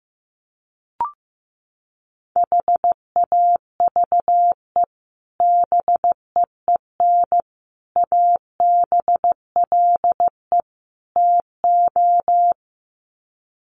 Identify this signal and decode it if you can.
Morse code, heard as a keyed tone, about 15 wpm, 700 Hz. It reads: HAVE BEEN ABLE TO
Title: Sets of 4 Words 15wpm